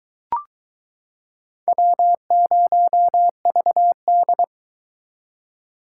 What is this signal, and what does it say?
Morse code, heard as a keyed tone, about 21 wpm, 700 Hz. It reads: W0VD